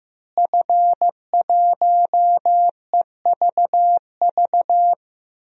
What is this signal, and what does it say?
Morse code, heard as a keyed tone, about 15 wpm, 700 Hz. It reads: F1EVV
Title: DXCC-100 Call Signs - 5 Characters - Repeated 2x - 15wpm